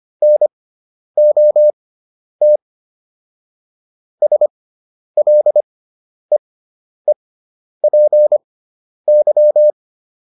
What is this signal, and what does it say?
Morse code, heard as a keyed tone, about 25 wpm, spaced as for 10 wpm, 600 Hz. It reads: NOT SLEEPY